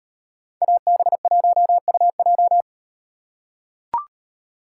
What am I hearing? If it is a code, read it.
Morse code, heard as a keyed tone, about 38 wpm, 700 Hz. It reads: AB1UJ